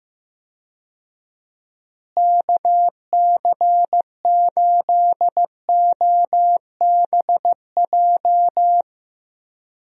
Morse code, heard as a keyed tone, about 15 wpm, 700 Hz. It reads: KC8OBJ
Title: USA Call Signs - 6 Characters - Repeated 2x - 15wpm